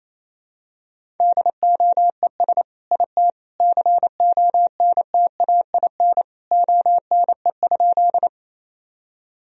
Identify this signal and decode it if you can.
Morse code, heard as a keyed tone, about 28 wpm, 700 Hz. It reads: DOES IT CONTAIN ONE?
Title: Sentences from Top 1000 Words - Review Part 2 - 28wpm